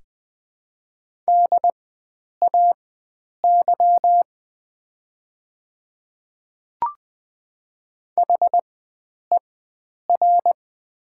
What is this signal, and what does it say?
Morse code, heard as a keyed tone, about 20 wpm, spaced as for 9 wpm, 700 Hz. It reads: DAY HER
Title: 4x Character Spacing - Top 100 Words - 20wpm